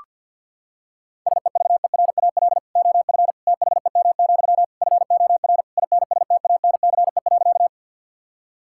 Morse code, heard as a keyed tone, about 48 wpm, 690 Hz. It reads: SEVERAL OF THEM, FOR INSTANCE.
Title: Sentences from Top 1000 Words - Review Part 5 - 50wpm